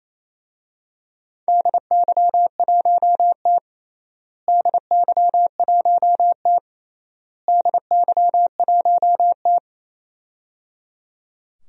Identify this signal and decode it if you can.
Morse code, heard as a keyed tone, about 28 wpm, 700 Hz. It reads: DY1T DY1T DY1T
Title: DXCC-100 Call Signs - 4 Characters - Repeated 3x - 28wpm